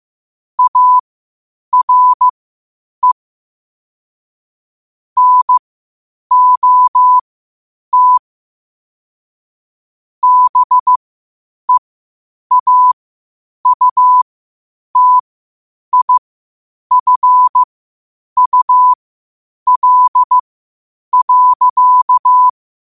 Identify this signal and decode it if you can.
Morse code, heard as a keyed tone, about 15 wpm, spaced as for 8 wpm, 1000 Hz. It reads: ARE NOT BEAUTIFUL.
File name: qotdfiles / 8wpm